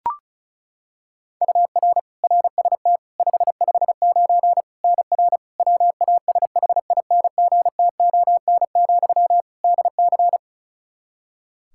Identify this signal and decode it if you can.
Morse code, heard as a keyed tone, about 34 wpm, 700 Hz. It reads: UR RST 559 NR WASHINGTON, DC
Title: K1USN SST QSO Practice 35wpm